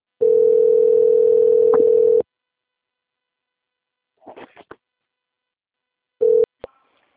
Ringback tone (North America), 440 + 480 Hz.